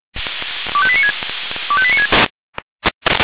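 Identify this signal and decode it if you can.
Five-tone selective call (ZVEI), heard as quick runs of five tones, tones of about 70 ms. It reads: 25896 25896
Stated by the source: ZVEI five-tone call 25433 received by SDR.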